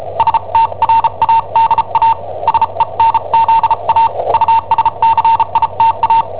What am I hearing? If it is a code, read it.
Morse code, heard as a keyed tone, about 34 wpm, spaced as for 30 wpm, 910 Hz. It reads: STRADA SENZA USCITA
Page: Cassetta esercizi preparata da I7OHP per IW5ELP